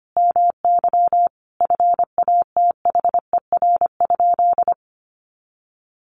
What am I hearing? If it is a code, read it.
Morse code, heard as a keyed tone, about 25 wpm, 700 Hz. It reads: MY FATHER?
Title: Sentences from Top 1400 Words 25wpm